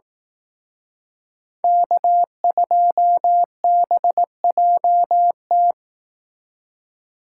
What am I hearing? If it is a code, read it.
Morse code, heard as a keyed tone, about 18 wpm, 700 Hz. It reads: K2BJT
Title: Speed Racing - USA 1x3 Call Signs 15wpm